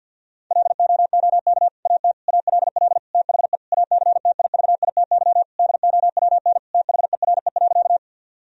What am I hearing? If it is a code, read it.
Morse code, heard as a keyed tone, about 50 wpm, 700 Hz. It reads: LOOK AT ALL THE ACTIVITY DOWN THERE.
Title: Sentences from Top 1000 Words - Review Part 4 - 50wpm